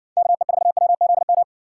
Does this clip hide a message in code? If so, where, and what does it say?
Yes: Morse code from 0.2 to 1.4 s, heard as a keyed tone, about 60 wpm, 700 Hz. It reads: KE4WZM